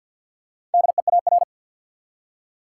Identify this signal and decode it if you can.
Morse code, heard as a keyed tone, about 52 wpm, 700 Hz. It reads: DEAR
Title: Four Letter Words - 50wpm